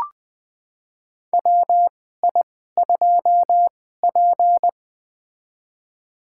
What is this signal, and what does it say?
Morse code, heard as a keyed tone, about 20 wpm, spaced as for 14 wpm, 700 Hz. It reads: WI2P
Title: USA Call Signs - 4 Characters - 2x Character Spacing - 20wpm